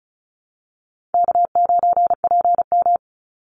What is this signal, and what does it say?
Morse code, heard as a keyed tone, about 34 wpm, 700 Hz. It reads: K9PM
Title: NA SOTA Chaser Callsigns 35wpm